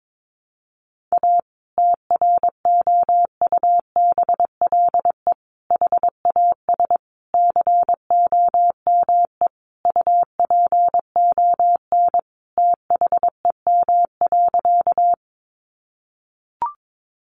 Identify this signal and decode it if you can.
Morse code, heard as a keyed tone, about 22 wpm, 700 Hz. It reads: A TROUBLE HAS COME UPON THEM.
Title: Sentences from Top 1200 Words 22wpm